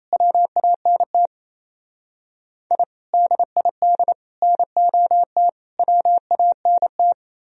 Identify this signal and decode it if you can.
Morse code, heard as a keyed tone, about 35 wpm, 700 Hz. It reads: WANT I DID NOT WANT
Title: Speed-Racing - Sets of 4 Words 28wpm